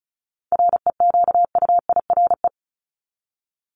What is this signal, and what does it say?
Morse code, heard as a keyed tone, about 34 wpm, 700 Hz. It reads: REQUIRE
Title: Top 751-1000 Words 35wpm